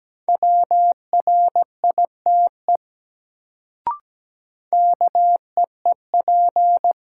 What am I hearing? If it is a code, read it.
Morse code, heard as a keyed tone, about 17 wpm, 700 Hz. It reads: WRITE KEEP
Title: Sentences from Top 300 Words - New Word Drill - 17wpm